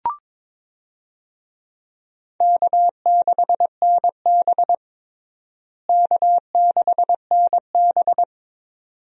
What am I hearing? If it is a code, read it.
Morse code, heard as a keyed tone, about 22 wpm, 700 Hz. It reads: K6NB K6NB
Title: USA Call Signs - 4 Characters - Repeated 2x - 22wpm